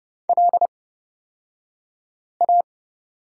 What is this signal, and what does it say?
Morse code, heard as a keyed tone, about 30 wpm, spaced as for 5 wpm, 700 Hz. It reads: LA
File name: Lesson 37d - Letter x - Two Characters 30wpm